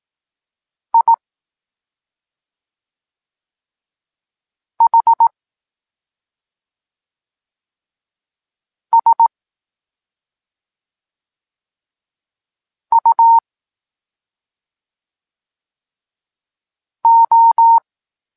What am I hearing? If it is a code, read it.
Morse code, heard as a keyed tone, about 18 wpm, spaced as for 2 wpm, 920 Hz. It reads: IHSUO
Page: Tekst 8 – doskonalenie odbioru poznanych liter